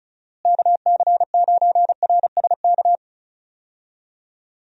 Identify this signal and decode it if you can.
Morse code, heard as a keyed tone, about 36 wpm, 700 Hz. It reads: KC9RSK
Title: USA Call Signs - 6 Characters - Repeated 2x - 35wpm